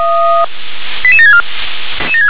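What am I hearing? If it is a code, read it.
Five-tone selective call (ZVEI), heard as quick runs of five tones, tones of about 70 ms. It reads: 88764
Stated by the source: Received on an SDR, ZVEI five-tone selective call.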